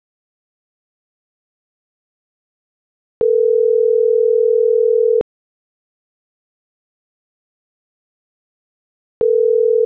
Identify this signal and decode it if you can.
Ringback tone (North America), 440 + 480 Hz.